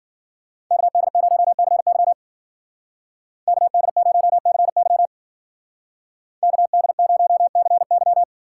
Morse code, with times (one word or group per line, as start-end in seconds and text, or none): 0.7-2.1 KD0CY
3.5-5.1 KD0CY
6.4-8.2 KD0CY